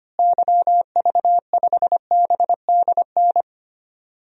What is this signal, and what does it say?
Morse code, heard as a keyed tone, about 25 wpm, 700 Hz. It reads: YV5BDN
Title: DXCC-100 Call Signs - 6 Characters - Repeated 2x - 25wpm